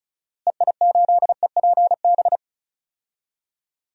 Morse code, heard as a keyed tone, about 36 wpm, 700 Hz. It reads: EI8EPB
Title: DXCC-100 Call Signs - 6 Characters - Repeated 2x - 35wpm